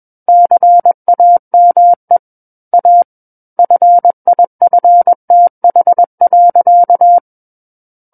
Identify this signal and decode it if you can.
Morse code, heard as a keyed tone, about 21 wpm, 700 Hz. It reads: CAME A FIFTH.
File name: Book One / 12-Chapter 12 - 21 WPM